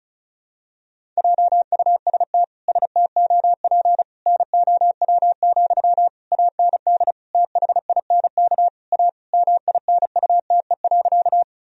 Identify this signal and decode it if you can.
Morse code, heard as a keyed tone, about 36 wpm, 700 Hz. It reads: JUST STOP NOW, AND THINK A MINUTE.
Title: Sentences from Top 1000 Words - Review Part 1 - 35wpm